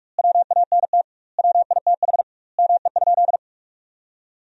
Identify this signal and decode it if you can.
Morse code, heard as a keyed tone, about 46 wpm, 700 Hz. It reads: WANT WITH ME?
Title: 2x Word-Spacing - Sentences from Top 200 Words - 45wpm